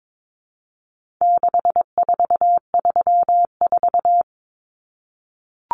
Morse code, heard as a keyed tone, about 22 wpm, 700 Hz. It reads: 6434
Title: Numbers - 4-Digit 22wpm